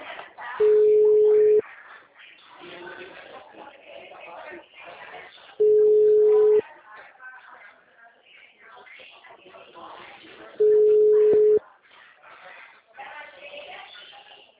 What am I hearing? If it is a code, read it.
Ringback tone (Europe, 425 Hz), with speech.
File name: out- -2019- - - .9911